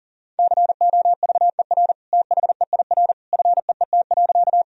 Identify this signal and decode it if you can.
Morse code, heard as a keyed tone, about 40 wpm, 700 Hz. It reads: COVER THEIR FEET.
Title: Sentences from Top 1000 Words - Review Part 1 - 40wpm